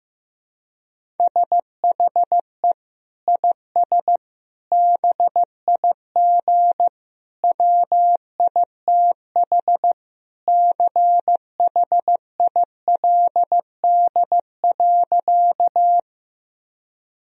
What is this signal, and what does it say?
Morse code, heard as a keyed tone, about 16 wpm, 700 Hz. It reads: SHE IS BIG WITH CHILD.
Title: Sentences from Top 1000 Words - Review Part 1 - 15wpm